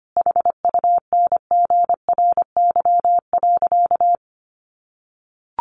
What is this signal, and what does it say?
Morse code, heard as a keyed tone, about 25 wpm, 700 Hz. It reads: HUNGRY.
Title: Sentences from Top 1400 Words 25wpm